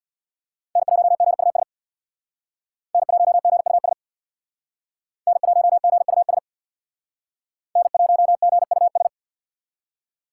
Morse code, heard as a keyed tone, about 62 wpm, 700 Hz. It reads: N1GUS N1GUS N1GUS N1GUS
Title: Speed Racing - USA 1x3 Call Signs 50wpm